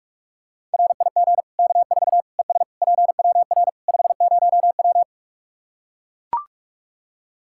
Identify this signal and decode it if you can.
Morse code, heard as a keyed tone, about 41 wpm, 700 Hz. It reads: RIG K4 ES PWR 50W
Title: Practice QSO - Exchange 2 - Info - 45wpm